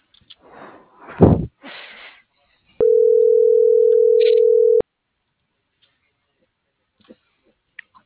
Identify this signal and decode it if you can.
Ringback tone (North America), 440 + 480 Hz.